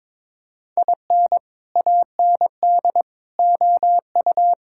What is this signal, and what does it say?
Morse code, heard as a keyed tone, about 22 wpm, 700 Hz. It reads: IN AND OU
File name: practice / 22wpm / Sentences with 4-Word Phrases - 22wpm